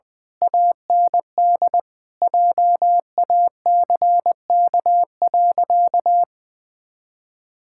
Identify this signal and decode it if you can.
Morse code, heard as a keyed tone, about 20 wpm, 700 Hz. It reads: AND JACK.
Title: Pangram Challenge v2 - 20wpm